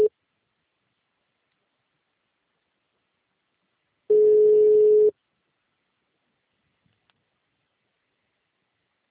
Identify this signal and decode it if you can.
Ringback tone (Europe, 425 Hz).